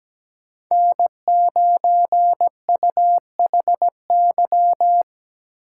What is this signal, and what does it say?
Morse code, heard as a keyed tone, about 17 wpm, 700 Hz. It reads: N9UHY
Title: USA Call Signs - 5 Characters - 4x Character Spacing - 17wpm